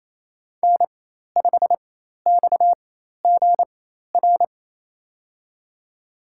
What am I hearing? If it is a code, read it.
Morse code, heard as a keyed tone, about 27 wpm, spaced as for 13 wpm, 700 Hz. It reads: N5XGR
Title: USA Call Signs - 5 Characters - 4x Character Spacing - 28wpm